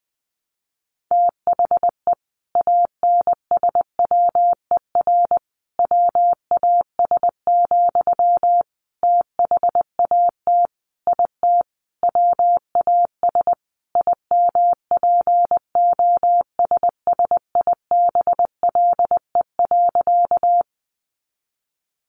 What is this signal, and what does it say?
Morse code, heard as a keyed tone, about 20 wpm, 700 Hz. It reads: THE ANSWER WAS, THAT IT WAS IMPOSSIBLE.
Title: Sentences from Top 1100 Words 20wpm